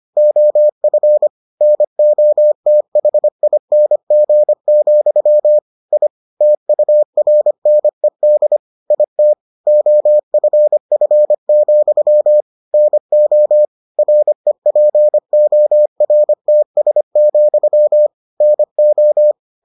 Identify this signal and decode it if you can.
Morse code, heard as a keyed tone, about 25 wpm, 600 Hz. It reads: OF NOTHING, I TURNED IT OFF, NO REPORTS, NO